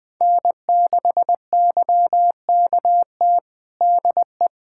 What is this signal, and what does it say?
Morse code, heard as a keyed tone, about 20 wpm, 700 Hz. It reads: N6YKT DE
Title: USA Callsign DE Callsign - 5 Characters - 20wpm